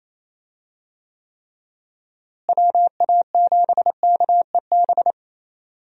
Morse code, heard as a keyed tone, about 28 wpm, 700 Hz. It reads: WA7KEB
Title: USA Call Signs - 6 Characters - Repeated 2x - 28wpm